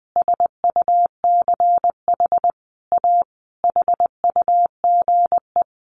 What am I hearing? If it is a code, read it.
Morse code, heard as a keyed tone, about 20 wpm, 700 Hz. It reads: SUCH A HUGE
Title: Sentences from Top 1100 Words 20wpm